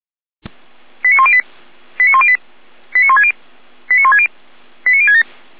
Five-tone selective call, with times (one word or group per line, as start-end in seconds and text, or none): 1.0-1.4 89108
2.0-2.4 89108
2.9-3.3 89160
3.9-4.3 89160
4.9-5.2 89067